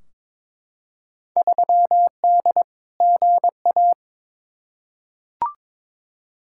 Morse code, heard as a keyed tone, about 22 wpm, 700 Hz. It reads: 3D GA
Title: Field Day Exchanges Encore - 22wpm